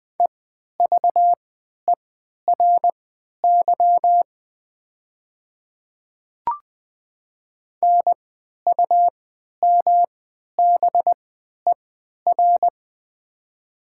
Morse code, heard as a keyed tone, about 20 wpm, spaced as for 11 wpm, 700 Hz. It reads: EVERY NUMBER
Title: 3x Character Spacing - Top 300 Words - 20wpm